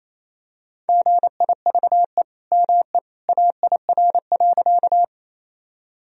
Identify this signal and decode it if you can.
Morse code, heard as a keyed tone, about 28 wpm, 700 Hz. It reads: GIVE ME AIR.
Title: Sentences from Top 1000 Words - Review Part 1 - 28wpm